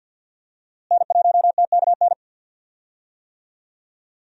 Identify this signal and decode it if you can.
Morse code, heard as a keyed tone, about 52 wpm, 700 Hz. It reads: N1TKN
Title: USA Call Signs - 5 Characters - Repeated 2x - 50wpm